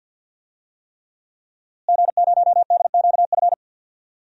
Morse code, heard as a keyed tone, about 47 wpm, 700 Hz. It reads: G0DQF
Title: DXCC-100 Call Signs - 5 Characters - Repeated 3x - 50wpm